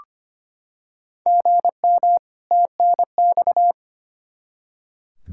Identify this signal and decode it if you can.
Morse code, heard as a keyed tone, about 25 wpm, 700 Hz. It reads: GM TNX
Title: Practice QSO - Exchange 1 - Beginning - 25wpm